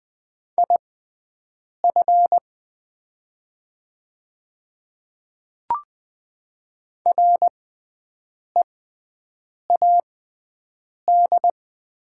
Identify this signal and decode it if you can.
Morse code, heard as a keyed tone, about 20 wpm, spaced as for 7 wpm, 700 Hz. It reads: IF READ
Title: 6x Character Spacing - Top 500 Words - 20wpm